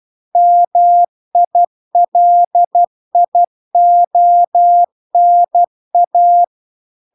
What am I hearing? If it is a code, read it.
Morse code, heard as a keyed tone, about 12 wpm, 700 Hz. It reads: MILIONA